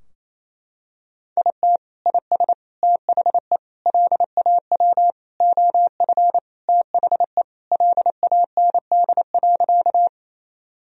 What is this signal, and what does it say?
Morse code, heard as a keyed tone, about 27 wpm, 700 Hz. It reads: IT IS THE LAW OF THE LAND.
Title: Sentences from Top 1000 Words - Review Part 3 - 28wpm